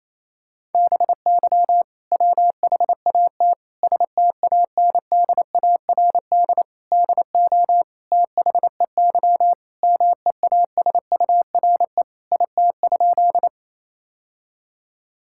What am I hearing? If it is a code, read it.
Morse code, heard as a keyed tone, about 28 wpm, 700 Hz. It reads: BY WHAT STANDARD DO THEY MEASURE IT?
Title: Sentences from Top 1000 Words - Review Part 6 - 28wpm